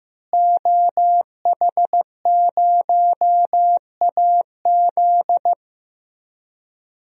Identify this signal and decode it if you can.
Morse code, heard as a keyed tone, about 16 wpm, 700 Hz. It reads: OH0AZ
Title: DXCC-100 Call Signs - 5 Characters - Repeated 3x - 15wpm